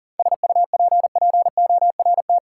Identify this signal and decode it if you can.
Morse code, heard as a keyed tone, about 40 wpm, 700 Hz. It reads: SUPPORT
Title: Sentences from Top 800 Words - New Word Drill - 40wpm